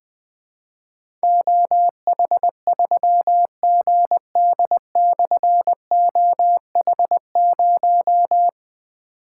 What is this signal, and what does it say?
Morse code, heard as a keyed tone, about 21 wpm, 700 Hz. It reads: OH3GD/OH0
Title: Difficult Call Signs 20wpm